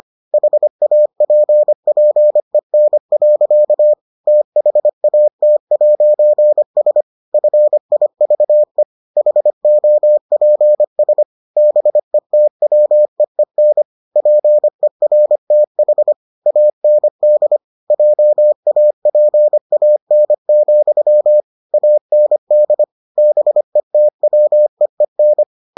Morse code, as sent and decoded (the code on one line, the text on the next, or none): .... .- .--. .--. -. .-.-.- / - .... .- - .----. ... / ..-. .. ...- . / .... --- .--. ... / -... - .-- . -. / .--. .-. - .... / .- -. -.. / .--- .- .--. .- -. --..-- / .- -. -.. / -... - .-- . -.
HAPPEN. THAT'S FIVE HOPS BETWEEN PERTH AND JAPAN, AND BETWEEN